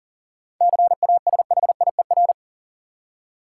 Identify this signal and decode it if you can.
Morse code, heard as a keyed tone, about 40 wpm, 700 Hz. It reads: CASHIER
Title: Practice QSO - Exchange 3 - More Info - 40wpm